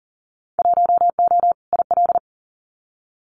Morse code, heard as a keyed tone, about 42 wpm, 700 Hz. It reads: 1O IL
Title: Winter Field Day Exchanges 40wpm